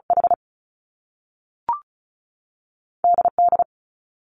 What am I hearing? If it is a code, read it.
Morse code, heard as a keyed tone, about 36 wpm, spaced as for 6 wpm, 700 Hz. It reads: HDD
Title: Top Two Letter Combinations in Words 35wpm